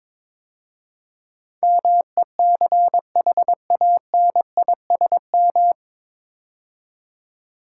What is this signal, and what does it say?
Morse code, heard as a keyed tone, about 22 wpm, 700 Hz. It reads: MECHANISM
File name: 9 Letter Words - First 3 Letters Given - 22wpm